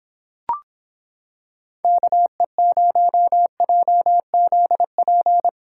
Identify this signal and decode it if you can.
Morse code, heard as a keyed tone, about 26 wpm, 700 Hz. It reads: KE0JZP